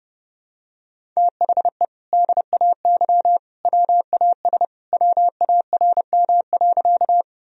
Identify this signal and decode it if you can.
Morse code, heard as a keyed tone, about 32 wpm, 700 Hz. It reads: THE DAY WAS WARM.
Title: Sentences from Top 1000 Words - Review Part 2 - 30wpm